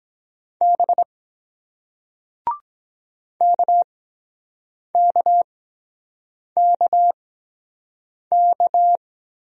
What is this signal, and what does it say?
Morse code, heard as a keyed tone, about 23 wpm, spaced as for 7 wpm, 700 Hz. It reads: B KKKK